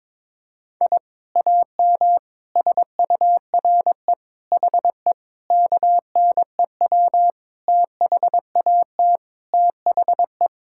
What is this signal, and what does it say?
Morse code, heard as a keyed tone, about 22 wpm, 700 Hz. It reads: I AM SURE HE KNEW THAT THE
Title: Sentences from Top 1000 Words - Review Part 6 - 22wpm